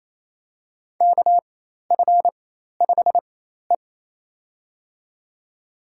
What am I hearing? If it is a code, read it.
Morse code, heard as a keyed tone, about 28 wpm, spaced as for 13 wpm, 700 Hz. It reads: KF5E
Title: USA Call Signs - 4 Characters - 4x Character Spacing - 28wpm